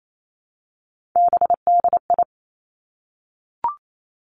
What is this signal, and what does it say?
Morse code, heard as a keyed tone, about 28 wpm, 700 Hz. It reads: BDI
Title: Three Challenging Letter-Number Combinations 28wpm